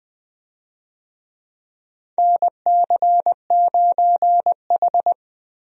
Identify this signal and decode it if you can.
Morse code, heard as a keyed tone, about 20 wpm, 700 Hz. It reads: NC9H
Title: USA Call Signs - 4 Characters - Repeated 3x - 20wpm